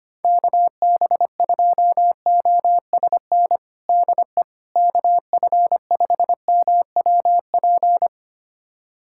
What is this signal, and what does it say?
Morse code, heard as a keyed tone, about 25 wpm, 700 Hz. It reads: KB2OSN DE KF5MWP
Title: USA Callsign DE Callsign 25wpm